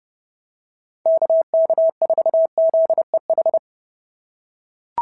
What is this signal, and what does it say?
Morse code, heard as a keyed tone, about 29 wpm, 660 Hz. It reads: KK4ZEH
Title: SOTA QSO Practice 30wpm